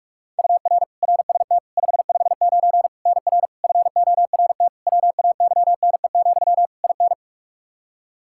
Morse code, heard as a keyed tone, about 46 wpm, 700 Hz. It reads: UR RST 559 NR FORT WAYNE, IN
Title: Practice QSO - Exchange 1 - Beginning - 45wpm